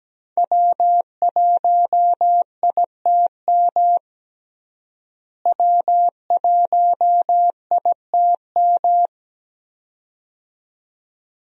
Morse code, heard as a keyed tone, about 17 wpm, 700 Hz. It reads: W1ITM W1ITM
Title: USA Call Signs - 5 Characters - Repeated 2x - 17wpm